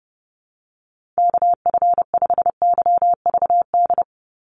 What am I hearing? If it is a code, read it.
Morse code, heard as a keyed tone, about 30 wpm, 700 Hz. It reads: KF5YVD
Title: USA Call Signs 30wpm